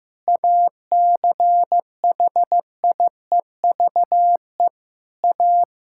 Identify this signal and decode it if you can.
Morse code, heard as a keyed tone, about 15 wpm, 700 Hz. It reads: ACHIEVE A
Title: Sentences from Top 1000 Words - Review Part 6 - 15wpm